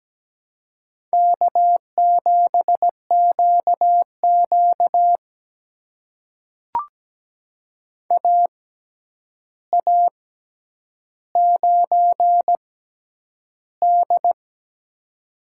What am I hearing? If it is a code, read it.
Morse code, heard as a keyed tone, about 17 wpm, spaced as for 9 wpm, 700 Hz. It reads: K7QQ AA9D
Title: USA Call Signs - 4 Characters - 6x Character Spacing - 17wpm